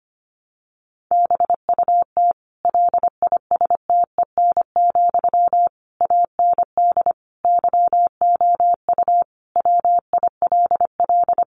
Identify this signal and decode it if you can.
Morse code, heard as a keyed tone, about 25 wpm, 700 Hz. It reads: BUT LISTEN, AND YOU WILL
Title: Sentences from Top 600 Words 25wpm